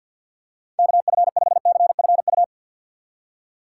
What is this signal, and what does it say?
Morse code, heard as a keyed tone, about 52 wpm, 700 Hz. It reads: KF5CFU
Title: USA Call Signs - 6 Characters - 4x Character Spacing - 50wpm